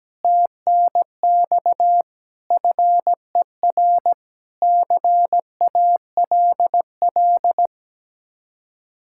Morse code, heard as a keyed tone, about 17 wpm, 700 Hz. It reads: TNX FER CALL
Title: Speed-Racing Common QSO Phrases - 17wpm